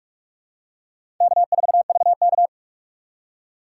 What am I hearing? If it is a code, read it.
Morse code, heard as a keyed tone, about 46 wpm, 700 Hz. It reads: K4VK
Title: USA Call Signs - 4 Characters - 2x Character Spacing - 45wpm